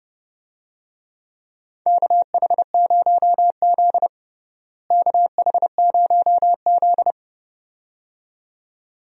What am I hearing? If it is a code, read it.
Morse code, heard as a keyed tone, about 30 wpm, 700 Hz. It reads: KH0Z KH0Z
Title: USA Call Signs - 4 Characters - Repeated 2x - 30wpm